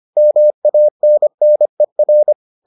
Morse code, heard as a keyed tone, about 25 wpm, 600 Hz. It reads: MANNER